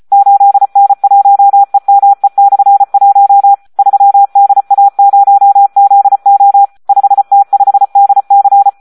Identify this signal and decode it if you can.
Morse code, heard as a keyed tone, about 34 wpm, 800 Hz. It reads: 8N1EME/1 3DA0ZO 5T5DC
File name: Audio practica CW / SD 1 34